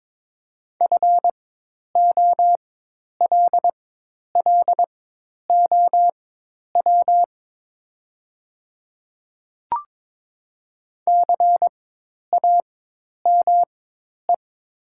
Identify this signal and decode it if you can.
Morse code, heard as a keyed tone, about 22 wpm, spaced as for 10 wpm, 700 Hz. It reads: FOLLOW CAME
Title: 4x Character Spacing - Top 200 Words - 22wpm